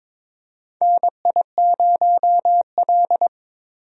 Morse code, heard as a keyed tone, about 22 wpm, 700 Hz. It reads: NI0L
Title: USA Call Signs - 4 Characters - Repeated 2x - 22wpm